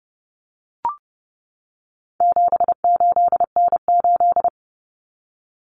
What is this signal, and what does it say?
Morse code, heard as a keyed tone, about 30 wpm, 700 Hz. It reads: 78N8
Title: Common Cut Numbers - 4-Digit 30wpm